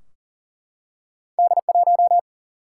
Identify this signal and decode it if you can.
Morse code, heard as a keyed tone, about 42 wpm, 700 Hz. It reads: D1
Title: Single-Letter Single-Number 40wpm